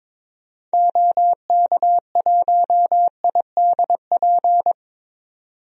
Morse code, heard as a keyed tone, about 22 wpm, 700 Hz. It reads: OK1IDP
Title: DXCC-100 Call Signs - 6 Characters - 22wpm